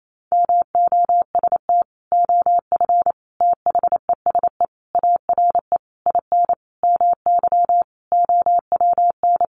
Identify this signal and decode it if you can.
Morse code, heard as a keyed tone, about 28 wpm, 700 Hz. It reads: MOST OF THESE ARE IN MY OWN
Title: Sentences from Top 1200 Words 28wpm